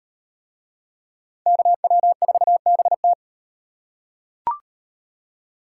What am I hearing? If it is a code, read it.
Morse code, heard as a keyed tone, about 38 wpm, 700 Hz. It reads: KW4BT